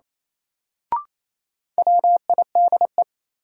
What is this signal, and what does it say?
Morse code, heard as a keyed tone, about 28 wpm, 700 Hz. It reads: WIDE
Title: Sentences from Top 900 Words - New Word Drill - 28wpm